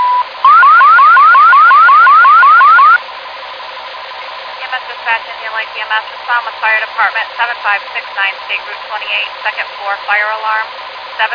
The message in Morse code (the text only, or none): none